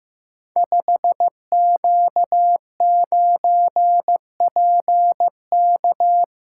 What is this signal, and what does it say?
Morse code, heard as a keyed tone, about 15 wpm, 700 Hz. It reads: 5Q9PK
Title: Code Groups - 5 Letter:Numbers - 15wpm